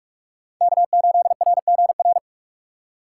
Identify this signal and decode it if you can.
Morse code, heard as a keyed tone, about 41 wpm, 700 Hz. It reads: K8RGR